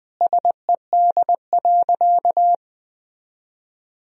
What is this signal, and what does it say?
Morse code, heard as a keyed tone, about 21 wpm, 700 Hz. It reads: SED.
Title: Sentences from Top 400 Words and Abbreviations 20wpm